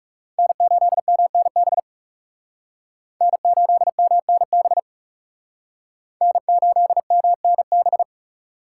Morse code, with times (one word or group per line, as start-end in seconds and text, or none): 0.4-1.8 N8MNB
3.2-4.8 N8MNB
6.2-8.0 N8MNB